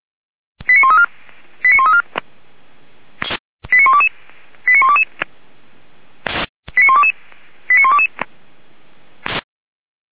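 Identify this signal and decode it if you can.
Five-tone selective call (ZVEI), heard as quick runs of five tones, tones of about 70 ms. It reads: 89134 89134 89122 89122 89122 89122
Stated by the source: Five-tone ZVEI selective call received on SDR.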